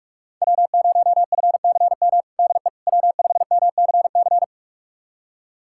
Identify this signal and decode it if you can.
Morse code, heard as a keyed tone, about 46 wpm, 700 Hz. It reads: W0FCM DE W5MCC
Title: USA Callsign DE Callsign - 5 Characters - 45wpm